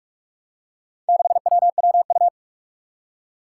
Morse code, heard as a keyed tone, about 46 wpm, 700 Hz. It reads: 6WWU
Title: Four Challenging Letter-Number Combinations 45wpm